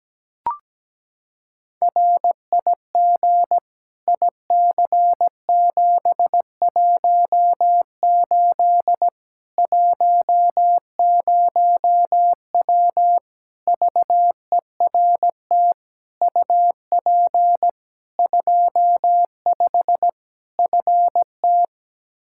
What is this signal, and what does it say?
Morse code, heard as a keyed tone, about 17 wpm, 700 Hz. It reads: RIG IC718 10W VERT UP 25 FT
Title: Practice QSO - Exchange 2 - Info - 17wpm